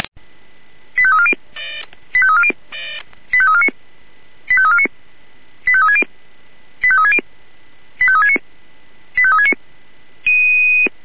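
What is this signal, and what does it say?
Five-tone selective call (ZVEI), heard as quick runs of five tones, tones of about 70 ms. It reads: 85340 85340 85349 85349 85360 85360 85369 85369
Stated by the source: Received by SDR.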